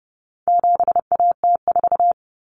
Morse code, heard as a keyed tone, about 30 wpm, 700 Hz. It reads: 7AT4
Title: Common Cut Numbers - 4-Digit 30wpm